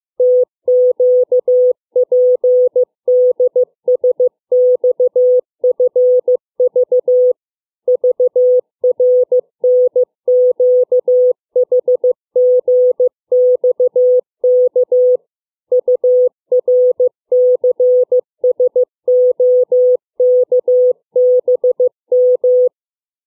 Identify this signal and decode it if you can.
Morse code, heard as a keyed tone, about 15 wpm, 500 Hz. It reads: TQPDSXFV VRNQHGXK URCSOKBM
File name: CW / letter number groups